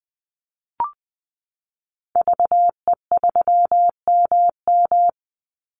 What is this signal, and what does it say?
Morse code, heard as a keyed tone, about 20 wpm, 700 Hz. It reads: VE3MM
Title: Call Signs Repeated 2x - 20wpm